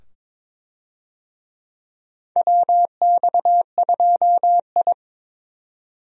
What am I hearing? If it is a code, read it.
Morse code, heard as a keyed tone, about 22 wpm, 700 Hz. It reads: WX2I
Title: USA Call Signs - 4 Characters - 22wpm